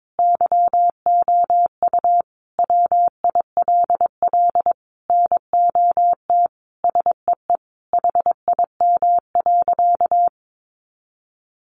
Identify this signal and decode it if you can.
Morse code, heard as a keyed tone, about 22 wpm, 700 Hz. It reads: YOU WILL NOT SEE HIM.
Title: Sentences from Top 600 Words 22wpm